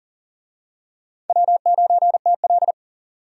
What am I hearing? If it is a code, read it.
Morse code, heard as a keyed tone, about 39 wpm, 700 Hz. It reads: W9TL
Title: USA Call Signs - 4 Characters - 40wpm